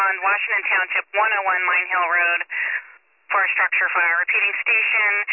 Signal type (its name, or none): Emergency Alert System (SAME) tones